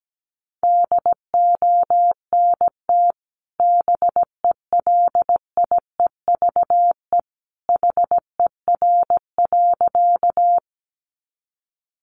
Morse code, heard as a keyed tone, about 17 wpm, 700 Hz. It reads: DONT BELIEVE HER.
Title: Sentences from Top 1000 Words 17wpm